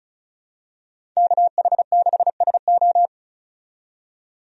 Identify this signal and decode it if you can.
Morse code, heard as a keyed tone, about 34 wpm, 700 Hz. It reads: KH6SO
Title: DXCC-100 Call Signs - 5 Characters - 35wpm